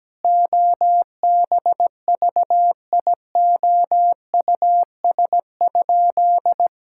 Morse code, heard as a keyed tone, about 17 wpm, 700 Hz. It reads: OBVIOUS?
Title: Sentences from Top 1600 Words 17wpm